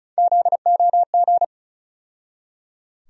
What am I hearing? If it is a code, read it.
Morse code, heard as a keyed tone, about 34 wpm, 700 Hz. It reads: ZOG
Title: USA Call Sign Postfixes - 3 Letters 35wpm